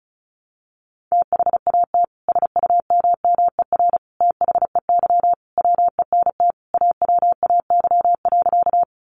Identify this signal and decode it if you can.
Morse code, heard as a keyed tone, about 34 wpm, 700 Hz. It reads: THAT SUMMER THEY WENT AWAY.
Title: Sentences from Top 1100 Words 35wpm